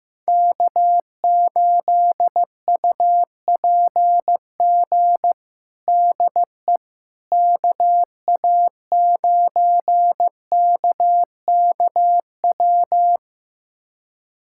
Morse code, heard as a keyed tone, about 15 wpm, 700 Hz. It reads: K8UPG DE KA9KKW
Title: USA Callsign DE Callsign 15wpm